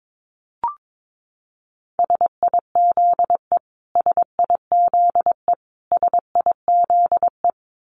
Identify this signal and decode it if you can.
Morse code, heard as a keyed tone, about 22 wpm, 700 Hz. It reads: SIZE SIZE SIZE
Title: Top 500 Words Repeated 22wpm